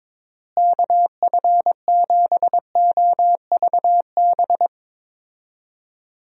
Morse code, heard as a keyed tone, about 22 wpm, 700 Hz. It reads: KF7OVB
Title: USA Call Signs - 6 Characters - Repeated 2x - 22wpm